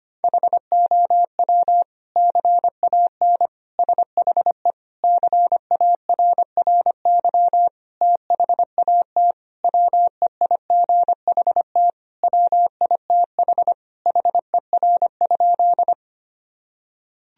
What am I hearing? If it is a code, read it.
Morse code, heard as a keyed tone, about 25 wpm, 700 Hz. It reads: HOW CAN SHE CARRY THAT WEIGHT WITH HER?
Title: Sentences from Top 1000 Words - Review Part 3 - 25wpm